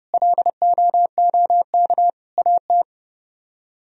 Morse code, heard as a keyed tone, about 38 wpm, 700 Hz. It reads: LOOK AT
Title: Speed-Racing - Sets of 4 Words 30wpm